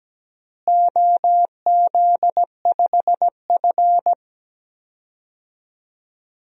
Morse code, heard as a keyed tone, about 17 wpm, 700 Hz. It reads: OZ5F
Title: DXCC-100 Call Signs - 4 Characters - Repeated 3x - 17wpm